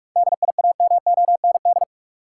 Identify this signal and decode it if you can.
Morse code, heard as a keyed tone, about 46 wpm, 700 Hz. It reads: DIAMOND
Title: Seven Letter Words - 45wpm